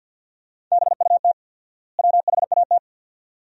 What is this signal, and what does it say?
Morse code, heard as a keyed tone, about 50 wpm, 700 Hz. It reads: BUT WHAT
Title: 4x Word-Spacing - Sentences from Top 100 Words - 50wpm